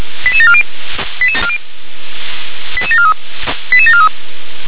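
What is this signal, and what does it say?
Five-tone selective call (ZVEI), heard as quick runs of five tones, tones of about 70 ms. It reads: 88740 88743 88743
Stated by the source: ZVEI five-tone selective call received with an SDR.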